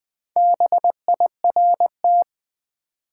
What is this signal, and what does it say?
Morse code, heard as a keyed tone, about 20 wpm, 700 Hz. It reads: BIRT
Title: Build A Word - 1 Letter At A Time - 5-Letter Words - 20wpm